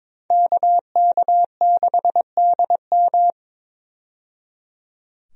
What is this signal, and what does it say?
Morse code, heard as a keyed tone, about 22 wpm, 700 Hz. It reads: KK6DM
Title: USA Call Signs - 5 Characters - 22wpm